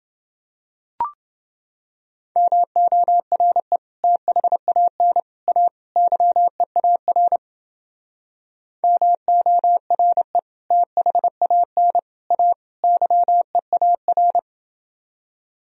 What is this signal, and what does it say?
Morse code, heard as a keyed tone, about 30 wpm, 700 Hz. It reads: MORE THAN A YEAR MORE THAN A YEAR